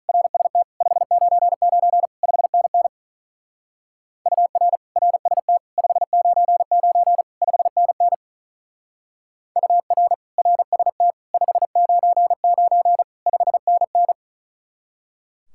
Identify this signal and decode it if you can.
Morse code, heard as a keyed tone, about 43 wpm, 700 Hz. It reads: RST 599 5NN UR RST 599 5NN UR RST 599 5NN
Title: Speed-Racing Common QSO Phrases - 35wpm